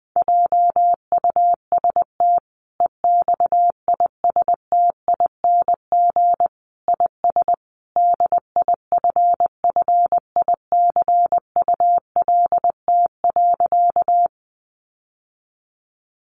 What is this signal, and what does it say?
Morse code, heard as a keyed tone, about 20 wpm, 700 Hz. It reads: JUST EXISTING IS DIFFICULT.
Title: Sentences from Top 1500 Words 20wpm